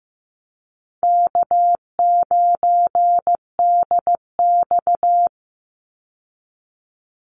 Morse code, heard as a keyed tone, about 15 wpm, 700 Hz. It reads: K9DX
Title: Call Signs Repeated 3x - 15wpm